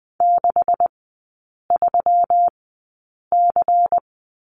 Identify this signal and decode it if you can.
Morse code, heard as a keyed tone, about 20 wpm, spaced as for 8 wpm, 700 Hz. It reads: 63C
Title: 138 Lesson 23e - Letter f - Three Characters 20wpm